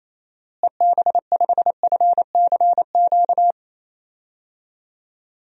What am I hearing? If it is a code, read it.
Morse code, heard as a keyed tone, about 28 wpm, 700 Hz. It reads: EB5FCQ
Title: DXCC-100 Call Signs - 6 Characters - Repeated 2x - 28wpm